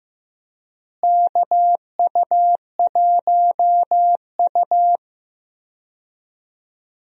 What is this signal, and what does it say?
Morse code, heard as a keyed tone, about 15 wpm, 700 Hz. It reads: KU1U
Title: USA Call Signs - 4 Characters - Repeated 2x - 15wpm